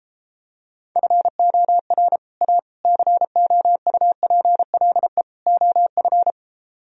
Morse code, heard as a keyed tone, about 33 wpm, 700 Hz. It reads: FOR A COUPLE OF